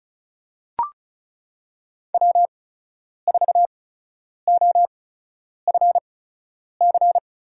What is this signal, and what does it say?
Morse code, heard as a keyed tone, about 36 wpm, spaced as for 10 wpm, 700 Hz. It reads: W4OFC
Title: USA Call Signs - 8x Character Spacing - 35wpm